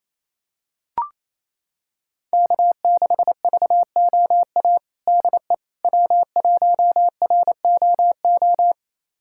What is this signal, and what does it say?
Morse code, heard as a keyed tone, about 27 wpm, 700 Hz. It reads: K6VOA DE W1ROO
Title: USA Callsign DE Callsign - 5 Characters - 28wpm